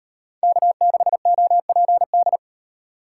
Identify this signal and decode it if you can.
Morse code, heard as a keyed tone, about 35 wpm, 700 Hz. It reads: K6OPD